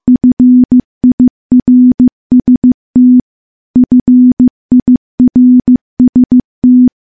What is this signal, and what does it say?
Morse code, heard as a keyed tone, about 15 wpm, 250 Hz. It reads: FIRST FIRST
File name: morse-training-15WPM